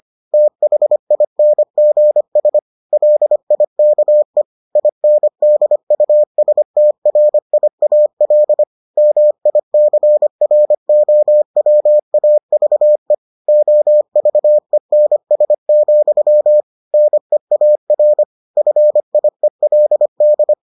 Morse code, heard as a keyed tone, about 25 wpm, 600 Hz. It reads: THINGS LIKE INDUSTRIAL MICROWAVE OVENS, NEAR FIELD